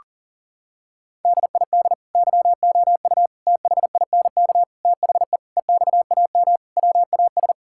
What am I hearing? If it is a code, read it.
Morse code, heard as a keyed tone, about 40 wpm, 700 Hz. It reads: DID YOU THINK THE EXAM WAS
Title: Sentences from Top 1000 Words - Review Part 5 - 40wpm